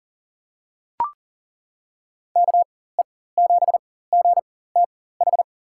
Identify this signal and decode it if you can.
Morse code, heard as a keyed tone, about 40 wpm, spaced as for 19 wpm, 700 Hz. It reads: KE7GTH
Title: USA Call Signs - 6 Characters - 4x Character Spacing - 40wpm